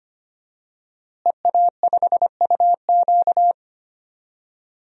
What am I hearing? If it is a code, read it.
Morse code, heard as a keyed tone, about 25 wpm, 700 Hz. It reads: EA5UQ
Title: DXCC-100 Call Signs - 5 Characters - Repeated 2x - 25wpm